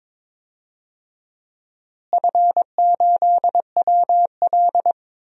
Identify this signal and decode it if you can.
Morse code, heard as a keyed tone, about 22 wpm, 700 Hz. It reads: F8WL
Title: DXCC-100 Call Signs - 4 Characters - Repeated 2x - 22wpm